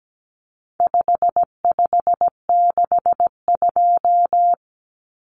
Morse code, heard as a keyed tone, about 17 wpm, 700 Hz. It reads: 5562
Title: Common Cut Numbers - 4-Digit 17wpm